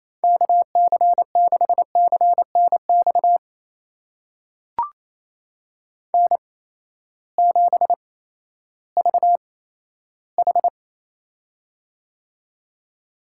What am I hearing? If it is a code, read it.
Morse code, heard as a keyed tone, about 28 wpm, spaced as for 14 wpm, 700 Hz. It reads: KC6CNX N7VH
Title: USA Call Signs - 8x Character Spacing - 28wpm